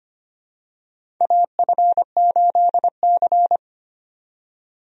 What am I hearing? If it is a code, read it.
Morse code, heard as a keyed tone, about 25 wpm, 700 Hz. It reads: AF8C
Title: USA Call Signs - 4 Characters - 25wpm